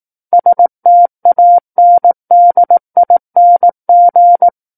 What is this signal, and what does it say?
Morse code, heard as a keyed tone, about 18 wpm, 700 Hz. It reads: STANDING
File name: Book One / 09-Chapter 9 - 18 WPM